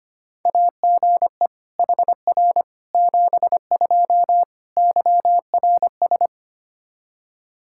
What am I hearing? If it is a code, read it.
Morse code, heard as a keyed tone, about 25 wpm, 700 Hz. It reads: AGE HR 72 YRS
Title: K1USN SST QSO Practice 25wpm